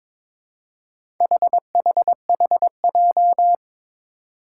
Morse code, heard as a keyed tone, about 21 wpm, 700 Hz. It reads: HHHJ
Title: Four Challenging Letter-Number Combinations 22wpm